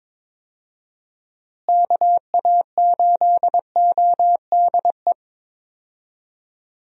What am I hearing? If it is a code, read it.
Morse code, heard as a keyed tone, about 22 wpm, 700 Hz. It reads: KA8ODE
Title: USA Call Signs - 6 Characters - Repeated 2x - 22wpm